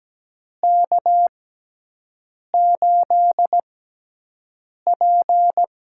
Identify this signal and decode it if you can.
Morse code, heard as a keyed tone, about 17 wpm, spaced as for 6 wpm, 700 Hz. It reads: K8P
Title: USA Call Signs - 6x Character Spacing - 17wpm